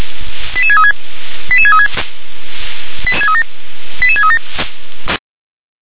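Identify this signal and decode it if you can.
Five-tone selective call (ZVEI), heard as quick runs of five tones, tones of about 70 ms. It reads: 88636 88636 88637
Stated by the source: ZVEI five-tone call received by SDR.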